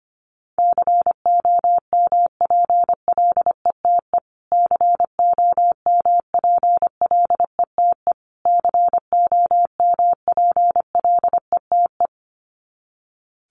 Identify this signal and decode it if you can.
Morse code, heard as a keyed tone, about 25 wpm, 700 Hz. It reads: COMPLETE COMPLETE COMPLETE
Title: Top 500 Words Repeated 25wpm